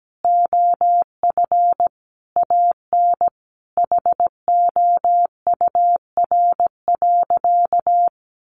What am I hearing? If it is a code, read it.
Morse code, heard as a keyed tone, about 17 wpm, 700 Hz. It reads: OF AN HOUR.
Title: Sentences from Top 1500 Words 17wpm